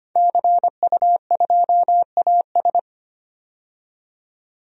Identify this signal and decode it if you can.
Morse code, heard as a keyed tone, about 25 wpm, 700 Hz. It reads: CU2AS
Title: DXCC-100 Call Signs - 5 Characters - 25wpm